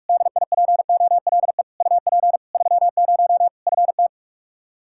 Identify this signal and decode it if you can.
Morse code, heard as a keyed tone, about 46 wpm, 700 Hz. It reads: DIPOLE UP 30 FT
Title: Practice QSO - Exchange 2 - Info - 45wpm